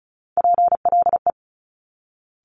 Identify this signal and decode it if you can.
Morse code, heard as a keyed tone, about 36 wpm, 700 Hz. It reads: PLE
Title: Top Three Letter Combinations in Words 35wpm